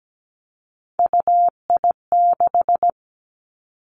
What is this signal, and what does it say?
Morse code, heard as a keyed tone, about 17 wpm, 700 Hz. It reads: UI6
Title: Three Challenging Letter-Number Combinations 17wpm